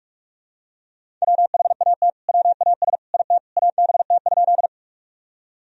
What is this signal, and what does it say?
Morse code, heard as a keyed tone, about 46 wpm, 700 Hz. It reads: WHAT WAS IT ABT?
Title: Sentences from Top 100 Words and Abbreviations 45wpm